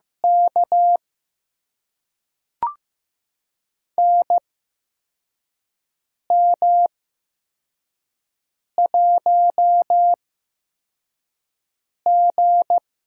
Morse code, heard as a keyed tone, about 15 wpm, spaced as for 4 wpm, 700 Hz. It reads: K NM1G
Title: USA Call Signs - 4 Characters - 8x Character Spacing - 15wpm